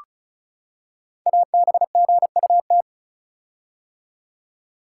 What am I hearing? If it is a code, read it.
Morse code, heard as a keyed tone, about 36 wpm, 700 Hz. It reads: ABGUT
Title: Code Groups - 5 Letters - 35wpm